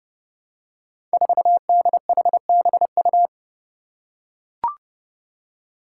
Morse code, heard as a keyed tone, about 30 wpm, 700 Hz. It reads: 4DHBU
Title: Five Challenging Letter-Number Combinations 30wpm